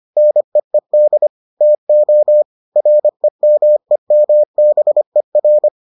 Morse code, heard as a keyed tone, about 25 wpm, 600 Hz. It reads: NEED TO REMEMBER